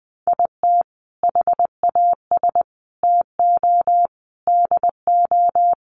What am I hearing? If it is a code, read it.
Morse code, heard as a keyed tone, about 20 wpm, 700 Hz. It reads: IT HAS TO DO
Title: Sets of 5 Words 20wpm